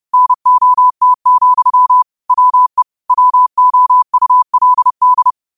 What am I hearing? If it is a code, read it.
Morse code, heard as a keyed tone, about 30 wpm, 1000 Hz. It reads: NOT, WE WOULD